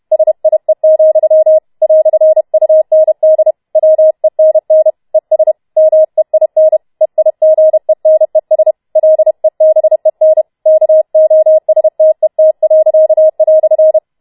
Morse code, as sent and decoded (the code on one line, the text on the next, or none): ... .. --..-- / .-..-. ..- -. -.. / .-- . -. -. / . ... / -- . .. -. / . .. --. -. ... / .-.. -... -. / -.- --- ... - . - .-.-.- .-..-.
SIE, "UND WENN ES MEIN EIGENES LEBEN KOSTET."